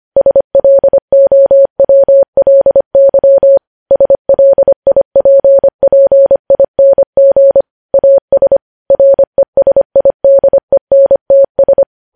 Morse code, heard as a keyed tone, about 25 wpm, 550 Hz. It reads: SLOWLY SLIPPING AS RESIDENTS
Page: News Headlines in Morse Code at 25 WPM